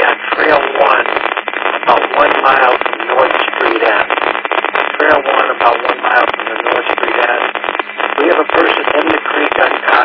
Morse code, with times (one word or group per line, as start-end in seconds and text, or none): none